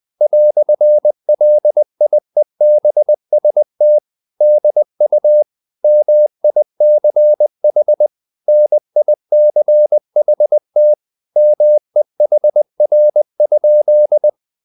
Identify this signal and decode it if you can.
Morse code, heard as a keyed tone, about 20 wpm, 600 Hz. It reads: "LIEBST DU MICH NICHT MEHR?
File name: dk5ke / Humor-100BpM